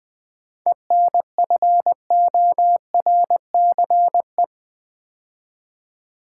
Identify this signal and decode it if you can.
Morse code, heard as a keyed tone, about 20 wpm, 700 Hz. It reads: ENFORCE
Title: 7 Letter Words - First 3 Letters Given - 20wpm